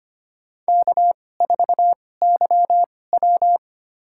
Morse code, heard as a keyed tone, about 25 wpm, spaced as for 18 wpm, 700 Hz. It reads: K4YW
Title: USA Call Signs - 4 Characters - 2x Character Spacing - 25wpm